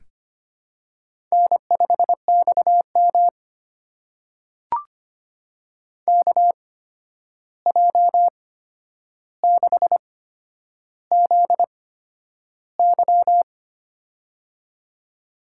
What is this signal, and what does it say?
Morse code, heard as a keyed tone, about 25 wpm, spaced as for 10 wpm, 700 Hz. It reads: N5XM KJ6ZY
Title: USA Call Signs - 8x Character Spacing - 25wpm